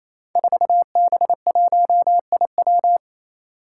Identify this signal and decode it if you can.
Morse code, heard as a keyed tone, about 28 wpm, 700 Hz. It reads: 4B1IW
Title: Five Challenging Letter-Number Combinations 28wpm